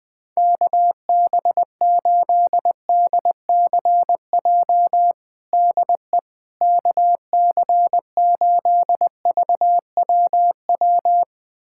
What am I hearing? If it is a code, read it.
Morse code, heard as a keyed tone, about 20 wpm, 700 Hz. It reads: KB8DCJ DE KC8VWW
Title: USA Callsign DE Callsign - 6 Characters - 20wpm